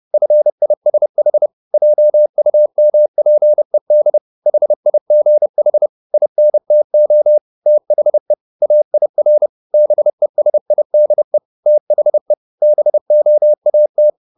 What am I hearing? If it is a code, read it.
Morse code, heard as a keyed tone, about 30 wpm, 600 Hz. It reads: FISH JUMPED HIGH INTO THE AIR BESIDE THE BOAT
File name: the old man and the sea / the old man and the sea 1 30wpm